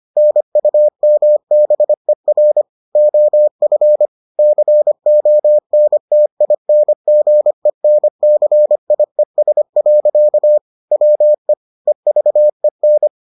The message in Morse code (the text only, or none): NUMBER OF CONTINGENCIES. WE EVEN